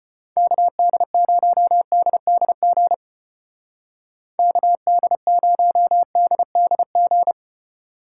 Morse code, heard as a keyed tone, about 31 wpm, 700 Hz. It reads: KD0DDG KD0DDG
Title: Speed Racing - USA 2x3 Call Signs 25wpm